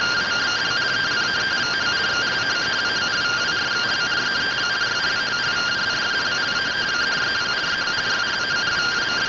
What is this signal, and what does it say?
Two-tone teleprinter signal, about 170 Hz shift.